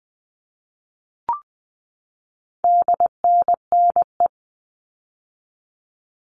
Morse code, heard as a keyed tone, about 20 wpm, 700 Hz. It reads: DNNE
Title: All Cut Numbers - 4-Digit 20wpm